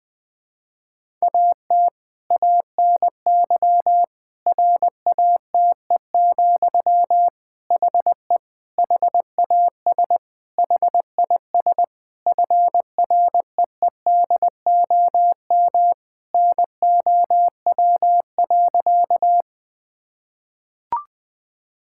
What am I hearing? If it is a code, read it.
Morse code, heard as a keyed tone, about 20 wpm, 700 Hz. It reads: AT ANY RATE, HE HAS HIS FREEDOM NOW.
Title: Sentences from Top 1000 Words - Review Part 5 - 20wpm